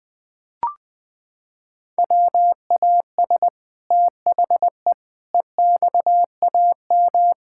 Morse code, heard as a keyed tone, about 21 wpm, 700 Hz. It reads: WAS THE EXAM
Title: Sentences from Top 1000 Words - Review Part 5 - 20wpm